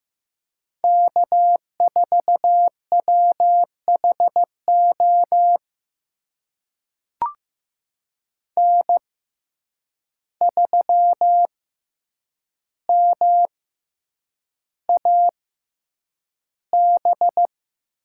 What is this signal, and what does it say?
Morse code, heard as a keyed tone, about 15 wpm, spaced as for 8 wpm, 700 Hz. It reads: K4WHO N3MAB
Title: USA Call Signs - 5 Characters - 6x Character Spacing - 15wpm